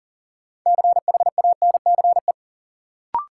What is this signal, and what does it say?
Morse code, heard as a keyed tone, about 40 wpm, 700 Hz. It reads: CHANCE
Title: Sets of 3 Words - New Word Drill - 40wpm